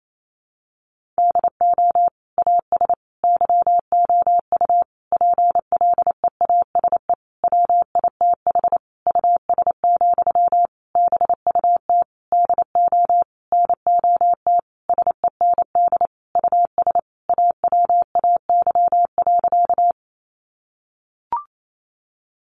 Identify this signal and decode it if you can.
Morse code, heard as a keyed tone, about 28 wpm, 700 Hz. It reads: DO AS YOU PLEASE WITH US, BUT DO NOT SEND US AWAY.
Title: Sentences from Top 1500 Words 28wpm